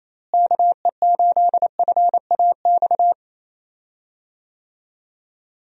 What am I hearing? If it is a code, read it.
Morse code, heard as a keyed tone, about 28 wpm, 700 Hz. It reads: KE8FAX
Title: USA Call Signs - 6 Characters - Repeated 3x - 28wpm